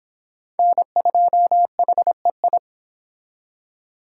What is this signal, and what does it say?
Morse code, heard as a keyed tone, about 26 wpm, 700 Hz. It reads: N2HEI